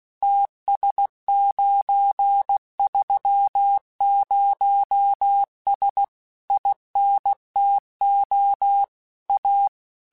Morse code, heard as a keyed tone, about 16 wpm, 800 Hz. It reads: TS930S INTO A